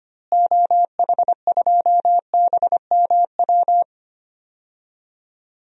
Morse code, heard as a keyed tone, about 25 wpm, 700 Hz. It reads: OH2BMW
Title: DXCC-100 Call Signs - 6 Characters - Repeated 2x - 25wpm